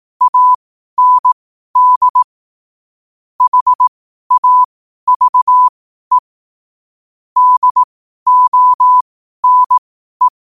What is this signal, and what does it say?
Morse code, heard as a keyed tone, about 18 wpm, spaced as for 12 wpm, 1000 Hz. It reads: AND HAVE DONE
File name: qotdfiles / 12wpm